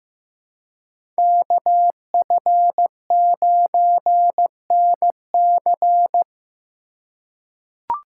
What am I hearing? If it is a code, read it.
Morse code, heard as a keyed tone, about 15 wpm, 700 Hz. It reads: KF9NC
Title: USA Call Signs - 5 Characters - 3x Character Spacing - 15wpm